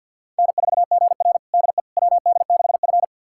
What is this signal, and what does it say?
Morse code, heard as a keyed tone, about 46 wpm, 700 Hz. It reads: N4GR DE WD6F
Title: USA Callsign DE Callsign - 4 Characters - 50wpm